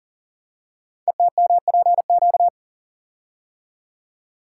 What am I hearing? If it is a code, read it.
Morse code, heard as a keyed tone, about 39 wpm, 690 Hz. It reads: ETMPQ
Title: Code Groups - 5 Letters - 40wpm